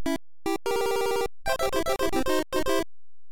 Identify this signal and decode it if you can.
There is no code.